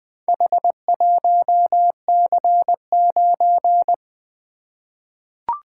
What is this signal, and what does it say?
Morse code, heard as a keyed tone, about 21 wpm, 700 Hz. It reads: H1C9
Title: Four Letter-Number Combinations - 20wpm